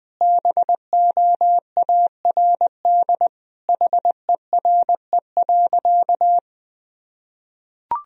Morse code, heard as a keyed tone, about 20 wpm, 700 Hz. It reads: BOARD HERE.
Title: Sentences from Top 1000 Words - Review Part 4 - 20wpm